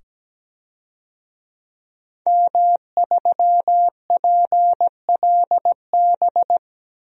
Morse code, heard as a keyed tone, about 17 wpm, 700 Hz. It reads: M3PLB
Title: DXCC-100 Call Signs - 5 Characters - 17wpm